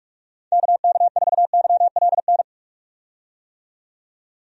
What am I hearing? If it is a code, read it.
Morse code, heard as a keyed tone, about 43 wpm, 700 Hz. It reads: KK4YLN
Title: USA Call Signs - 6 Characters - 45wpm